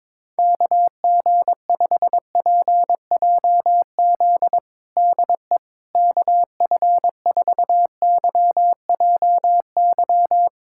Morse code, heard as a keyed tone, about 22 wpm, 700 Hz. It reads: KG5PJZ DE KF4YJY
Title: USA Callsign DE Callsign - 6 Characters - 22wpm